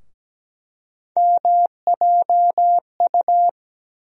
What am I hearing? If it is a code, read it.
Morse code, heard as a keyed tone, about 17 wpm, 700 Hz. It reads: MJU
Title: USA Call Sign Postfixes - 3 Letters 17wpm